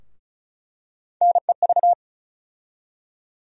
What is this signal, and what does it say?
Morse code, heard as a keyed tone, about 36 wpm, 700 Hz. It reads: NEV
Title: USA Call Sign Postfixes - 3 Letters 35wpm